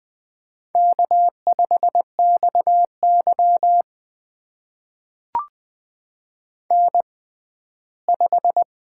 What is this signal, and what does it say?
Morse code, heard as a keyed tone, about 20 wpm, spaced as for 14 wpm, 700 Hz. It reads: K5XY N5
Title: USA Call Signs - 4 Characters - 6x Character Spacing - 20wpm